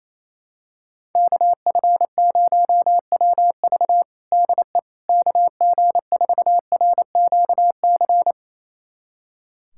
Morse code, heard as a keyed tone, about 28 wpm, 700 Hz. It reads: KF0WV DE KG4RQC
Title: USA Callsign DE Callsign 28wpm